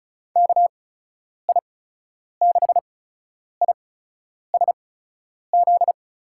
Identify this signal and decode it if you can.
Morse code, heard as a keyed tone, about 36 wpm, spaced as for 10 wpm, 700 Hz. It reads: KI6ISZ
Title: USA Call Signs - 6 Characters - 8x Character Spacing - 35wpm